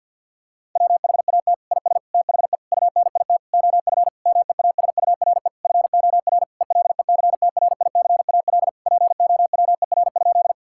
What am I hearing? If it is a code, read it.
Morse code, heard as a keyed tone, about 50 wpm, 700 Hz. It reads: WHAT IS THE UNIT OF MEASURE FOR ELECTRICAL POWER?
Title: Sentences from Top 1000 Words - Review Part 6 - 50wpm